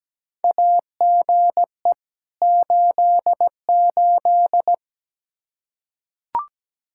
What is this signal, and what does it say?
Morse code, heard as a keyed tone, about 17 wpm, 700 Hz. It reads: AGE 88
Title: Practice QSO - Exchange 3 - More Info - 17wpm